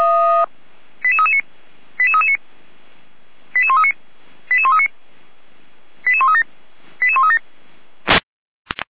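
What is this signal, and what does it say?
Five-tone selective call (ZVEI), heard as quick runs of five tones, tones of about 80 ms. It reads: 88339 88339 88139 88139 88137 88137
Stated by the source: ZVEI five-tone call received by SDR at Feuerwehr Orlamünde fire brigade.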